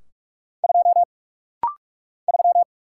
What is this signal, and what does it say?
Morse code, heard as a keyed tone, about 44 wpm, 700 Hz. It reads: 2 3
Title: Single Letter-Number - Rapid-Fire 45wpm